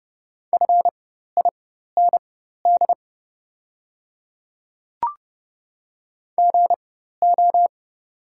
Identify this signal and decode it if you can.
Morse code, heard as a keyed tone, about 30 wpm, spaced as for 14 wpm, 700 Hz. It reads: FIND GO
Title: 4x Character Spacing - Top 100 Words - 30wpm